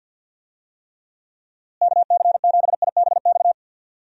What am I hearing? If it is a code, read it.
Morse code, heard as a keyed tone, about 46 wpm, 700 Hz. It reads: KC7IBX
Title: USA Call Signs - 6 Characters - Repeated 3x - 50wpm